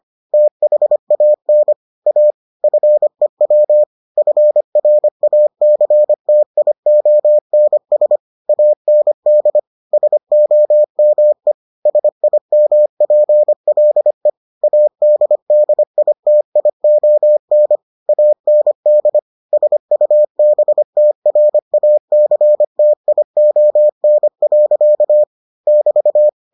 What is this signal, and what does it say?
Morse code, heard as a keyed tone, about 25 wpm, 600 Hz. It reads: THAN A FEW FRACTIONS AND SOME SIMPLE ADDITION AND SUBTRACTION. <BT>